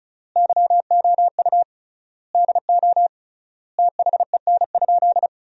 Morse code, heard as a keyed tone, about 36 wpm, 700 Hz. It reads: YOU DO THEN?
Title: 3x Word-Spacing - Sentences from Top 100 Words - 35wpm